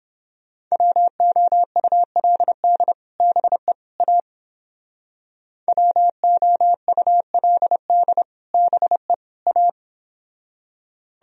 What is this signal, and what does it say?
Morse code, heard as a keyed tone, about 29 wpm, 700 Hz. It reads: WOULD BE A WOULD BE A